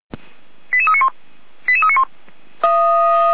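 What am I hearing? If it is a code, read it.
Five-tone selective call (ZVEI), heard as quick runs of five tones, tones of about 70 ms. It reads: 88391 88391
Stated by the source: ZVEI five-tone call received by SDR, on the audio server of Feuerwehr Orlamünde.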